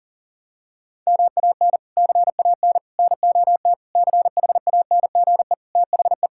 Morse code, heard as a keyed tone, about 40 wpm, 700 Hz. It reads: MAN CAN NOT CHANGE THE
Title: Sentences from Top 1000 Words - Review Part 5 - 40wpm